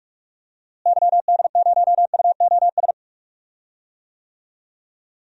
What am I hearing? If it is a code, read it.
Morse code, heard as a keyed tone, about 46 wpm, 700 Hz. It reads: YD0UOS
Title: DXCC-100 Call Signs - 6 Characters - Repeated 2x - 45wpm